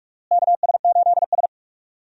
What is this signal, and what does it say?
Morse code, heard as a keyed tone, about 45 wpm, 700 Hz. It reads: KS8S
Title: USA Call Signs - 4 Characters - Repeated 3x - 45wpm